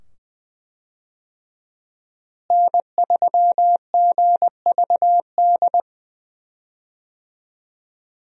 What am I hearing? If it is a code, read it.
Morse code, heard as a keyed tone, about 20 wpm, 700 Hz. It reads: N3GVD
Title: USA Call Signs - 5 Characters - Repeated 2x - 20wpm